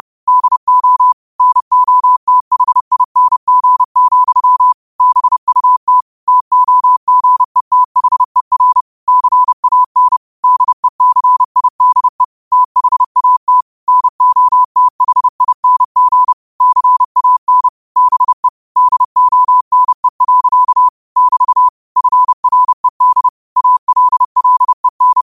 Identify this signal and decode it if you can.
Morse code, heard as a keyed tone, about 30 wpm, 1000 Hz. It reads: DO NOTHING, BUT TOGETHER CAN DECIDE THAT NOTHING CAN BE DONE. <BT> FRED ALLEN
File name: qotdfiles / 30wpm